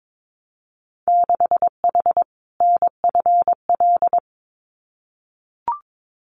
Morse code, heard as a keyed tone, about 22 wpm, 700 Hz. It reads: 6H NFL
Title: Winter Field Day Exchanges Encore 22wpm